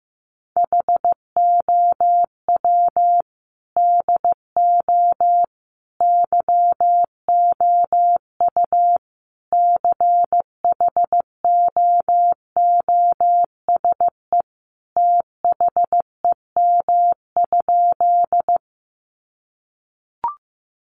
Morse code, heard as a keyed tone, about 15 wpm, 700 Hz. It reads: HOW DO YOU CHOOSE THEM?
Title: Sentences from Top 700 Words 15wpm